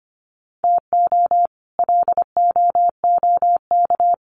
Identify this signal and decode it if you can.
Morse code, heard as a keyed tone, about 25 wpm, 700 Hz. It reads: TO LOOK
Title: Sets of 2 Words 25wpm